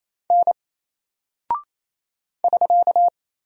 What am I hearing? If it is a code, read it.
Morse code, heard as a keyed tone, about 27 wpm, 700 Hz. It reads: N <SK>
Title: Single Character (Letter:Number:Punctuation:Prosign) 28wpm